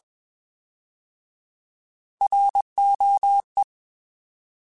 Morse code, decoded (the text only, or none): ROE